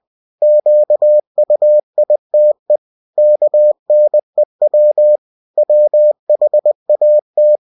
Morse code, heard as a keyed tone, about 20 wpm, 600 Hz. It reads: QUITE KNEW WHAT
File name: cw-books / winnie-the-pooh / winnie-20wpm-page-0103